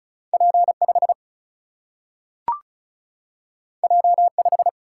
Morse code, heard as a keyed tone, about 36 wpm, 700 Hz. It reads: P 5 J 5
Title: Single-Letter Single-Number 35wpm